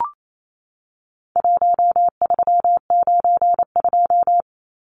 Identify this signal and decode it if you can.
Morse code, heard as a keyed tone, about 28 wpm, 700 Hz. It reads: 1392
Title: Numbers - Contest Style - 4-Digit 28wpm